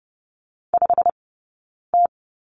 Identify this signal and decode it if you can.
Morse code, heard as a keyed tone, about 30 wpm, spaced as for 9 wpm, 690 Hz. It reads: 5T
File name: course / 30:10wpm / 086 Lesson 15d - Number 5 - Two Characters 30wpm